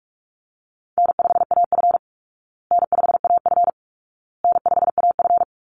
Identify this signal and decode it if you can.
Morse code, heard as a keyed tone, about 44 wpm, 700 Hz. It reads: N5AF N5AF N5AF
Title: Easy Call Signs Repeated 3x - 45wpm